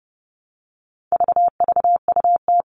Morse code, heard as a keyed tone, about 30 wpm, 700 Hz. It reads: VVUT
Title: All Cut Numbers - 4-Digit 30wpm